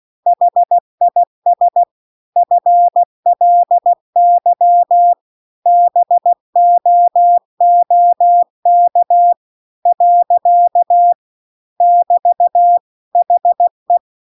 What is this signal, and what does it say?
Morse code, heard as a keyed tone, about 16 wpm, 700 Hz. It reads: HIS FLY BOOK . <BT> HE